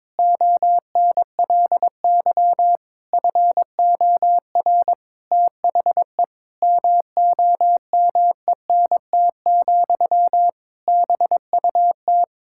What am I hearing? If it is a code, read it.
Morse code, heard as a keyed tone, about 22 wpm, 700 Hz. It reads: ONLY FOR THE MOMENT, BUT
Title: Sentences from Top 1000 Words - Review Part 6 - 22wpm